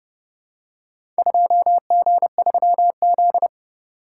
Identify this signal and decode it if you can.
Morse code, heard as a keyed tone, about 28 wpm, 700 Hz. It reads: 2G3Z